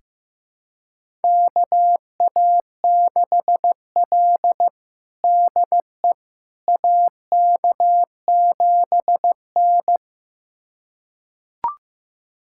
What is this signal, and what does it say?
Morse code, heard as a keyed tone, about 16 wpm, 700 Hz. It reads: KA6L DE AK7N
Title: USA Callsign DE Callsign - 4 Characters - 15wpm